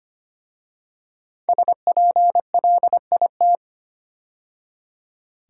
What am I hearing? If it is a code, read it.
Morse code, heard as a keyed tone, about 25 wpm, 700 Hz. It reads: SPLIT
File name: 5 Letter Words - First 3 Letters Given - 25wpm